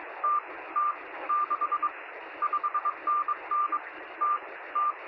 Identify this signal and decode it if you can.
Morse code, heard as a keyed tone, about 22 wpm, 1190 Hz, with radio noise. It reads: T T 6 5NN T T
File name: sm7lcb / 0325z es0s 23cm